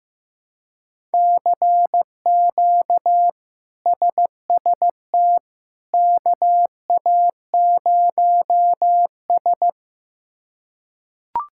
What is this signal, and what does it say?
Morse code, heard as a keyed tone, about 16 wpm, 700 Hz. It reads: CQ SST KA0S
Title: K1USN SST QSO Practice - 15wpm